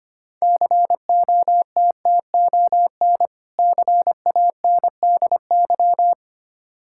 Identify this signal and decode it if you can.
Morse code, heard as a keyed tone, about 25 wpm, 690 Hz. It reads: COTTON CANDY
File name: practice / 25wpm / Difficult Words with Hints - 25wpm